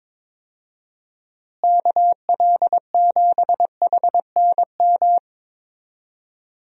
Morse code, heard as a keyed tone, about 22 wpm, 700 Hz. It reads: KL7HNM
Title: USA Call Signs - 6 Characters - Repeated 3x - 22wpm